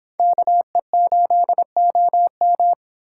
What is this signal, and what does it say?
Morse code, heard as a keyed tone, about 30 wpm, 700 Hz. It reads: KE8OM